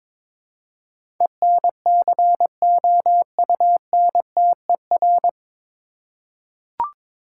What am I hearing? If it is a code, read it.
Morse code, heard as a keyed tone, about 22 wpm, 700 Hz. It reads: ENCOUNTER
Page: Morse Code Ninja practice 22wpm